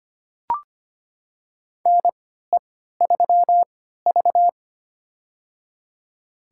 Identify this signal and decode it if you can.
Morse code, heard as a keyed tone, about 25 wpm, spaced as for 14 wpm, 700 Hz. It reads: NE3V
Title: USA Call Signs - 3x Character Spacing - 25wpm